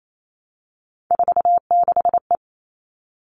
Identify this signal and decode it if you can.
Morse code, heard as a keyed tone, about 28 wpm, 700 Hz. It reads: 46E
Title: All Cut Numbers - 3-Digit 28wpm